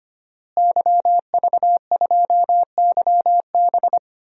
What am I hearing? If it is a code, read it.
Morse code, heard as a keyed tone, about 25 wpm, 700 Hz. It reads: YV2YB
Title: DXCC-100 Call Signs - 5 Characters - Repeated 3x - 25wpm